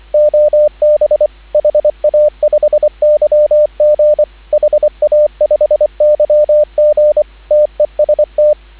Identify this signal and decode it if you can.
Morse code, heard as a keyed tone, about 24 wpm, 590 Hz, with radio noise. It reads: OB HA5YG HA5YG TEST